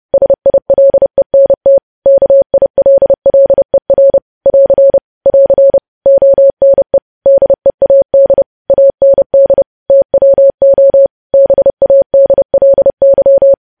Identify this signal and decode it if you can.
Morse code, heard as a keyed tone, about 30 wpm, 550 Hz. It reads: SILENT KILLER <AR> <AR> ONE DEAD AND TWO BADLY
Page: news headlines in morse code at 30 wpm